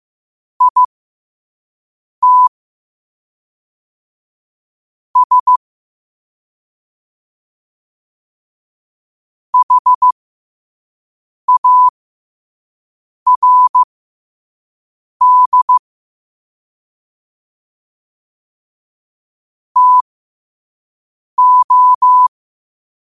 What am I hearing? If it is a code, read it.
Morse code, heard as a keyed tone, about 15 wpm, spaced as for 5 wpm, 1000 Hz. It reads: IT S HARD TO
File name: qotdfiles / 5wpm